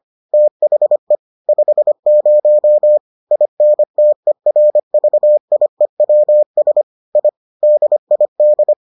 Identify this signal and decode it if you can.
Morse code, heard as a keyed tone, about 25 wpm, 600 Hz. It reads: THE 50 INTERVIEWS I DID